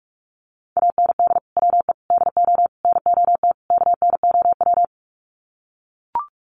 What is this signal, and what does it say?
Morse code, heard as a keyed tone, about 44 wpm, 700 Hz. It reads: AND WE DO NOT KNOW
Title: Sets of 5 Words Encore 45wpm